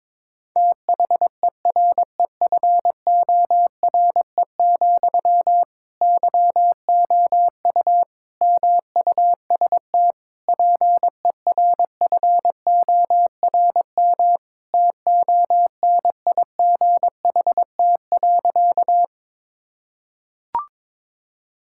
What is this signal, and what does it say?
Morse code, heard as a keyed tone, about 21 wpm, 700 Hz. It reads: THEREFORE, YOU MUST PERFORM TONIGHT.
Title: Sentences from Top 1000 Words - Review Part 5 - 22wpm